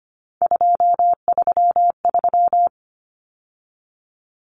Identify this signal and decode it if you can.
Morse code, heard as a keyed tone, about 25 wpm, 700 Hz. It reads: 233
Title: Numbers - 3-Digit 25wpm